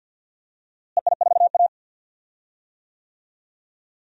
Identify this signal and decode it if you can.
Morse code, heard as a keyed tone, about 52 wpm, 700 Hz. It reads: EI4A
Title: DXCC-100 Call Signs - 4 Characters - Repeated 3x - 50wpm